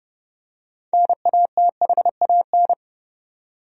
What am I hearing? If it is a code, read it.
Morse code, heard as a keyed tone, about 30 wpm, 700 Hz. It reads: NATHAN
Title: USA Common Names - 30wpm